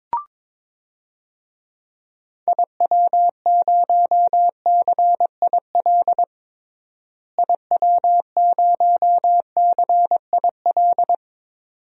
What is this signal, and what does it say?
Morse code, heard as a keyed tone, about 21 wpm, 700 Hz. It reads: IW0CIL IW0CIL
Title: DXCC-100 Call Signs - 6 Characters - Repeated 2x - 22wpm